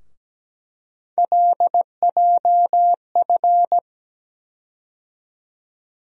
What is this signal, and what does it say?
Morse code, heard as a keyed tone, about 17 wpm, 700 Hz. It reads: LJF
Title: USA Call Sign Postfixes - 3 Letters 17wpm